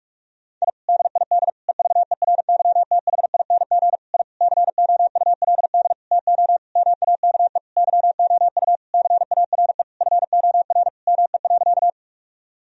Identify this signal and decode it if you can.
Morse code, heard as a keyed tone, about 45 wpm, 700 Hz. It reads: I DID EVERYTHING I COULD TO MAKE YOU CARE FOR ME.
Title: Sentences from Top 1000 Words - Review Part 1 - 45wpm